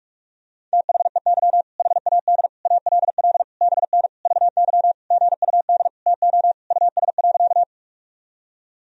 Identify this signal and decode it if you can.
Morse code, heard as a keyed tone, about 46 wpm, 700 Hz. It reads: THEY HAD ALL BN VY GUD TO US.
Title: Sentences from Top 200 Words and Abbreviations 45wpm